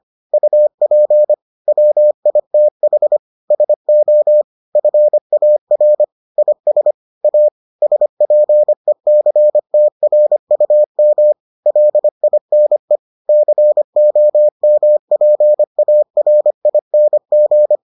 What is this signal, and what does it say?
Morse code, heard as a keyed tone, about 25 wpm, 600 Hz. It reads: UP WITH SO FAR IS A SPECTRUM LINE COMPARING